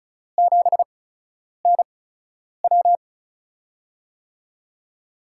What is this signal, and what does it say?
Morse code, heard as a keyed tone, about 36 wpm, spaced as for 10 wpm, 700 Hz. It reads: 7NW
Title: USA Call Signs - 5 Characters - 8x Character Spacing - 35wpm